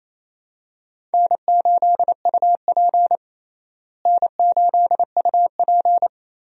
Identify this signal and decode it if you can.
Morse code, heard as a keyed tone, about 28 wpm, 700 Hz. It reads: N8UP N8UP
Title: USA Call Signs - 4 Characters - Repeated 3x - 28wpm